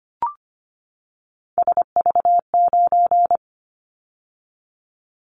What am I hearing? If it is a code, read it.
Morse code, heard as a keyed tone, about 25 wpm, 700 Hz. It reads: SV9
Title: Callsign Prefixes for Top 100 DXCC - 25wpm